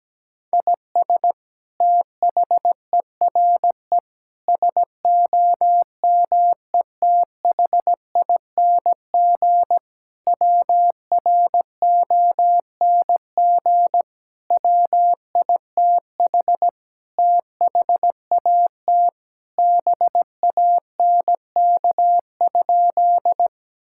Morse code, heard as a keyed tone, about 17 wpm, 700 Hz. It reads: IS THERE SOMETHING WRONG WITH THAT BANK?
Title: Sentences from Top 1000 Words - Review Part 6 - 17wpm